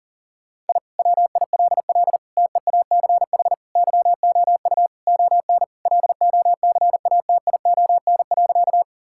Morse code, heard as a keyed tone, about 38 wpm, 700 Hz. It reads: I WILL TEACH YOU ON LOCATION.
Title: Sentences from Top 1000 Words - Review Part 6 - 40wpm